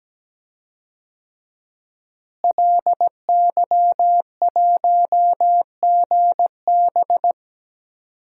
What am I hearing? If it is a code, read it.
Morse code, heard as a keyed tone, about 17 wpm, 700 Hz. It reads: LY1GB
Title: DXCC-100 Call Signs - 5 Characters - Repeated 3x - 17wpm